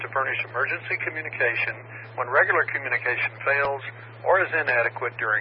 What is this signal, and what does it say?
Emergency Alert System (SAME) tones (header bursts), heard as harsh buzzing bursts, with speech. It is heard from 0.5 to 5.0 s.